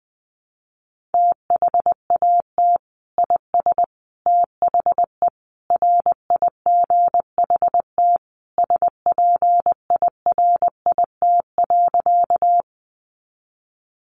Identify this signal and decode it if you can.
Morse code, heard as a keyed tone, about 20 wpm, 700 Hz. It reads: THAT IS THE RIGHT SPIRIT.
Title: Sentences from Top 1400 Words 20wpm